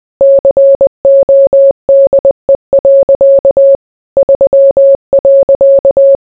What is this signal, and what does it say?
Morse code, heard as a keyed tone, about 20 wpm, 550 Hz. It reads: CODE. 3.